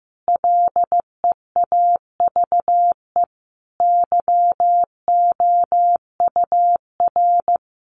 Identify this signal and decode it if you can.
Morse code, heard as a keyed tone, about 15 wpm, 700 Hz. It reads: LEAVE YOUR
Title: A Sound Of Thunder 15wpm